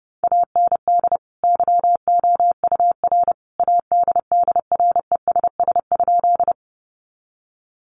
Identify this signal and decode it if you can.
Morse code, heard as a keyed tone, about 30 wpm, 700 Hz. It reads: AND YOUR ADDRESS?
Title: Sentences from Top 1000 Words 30wpm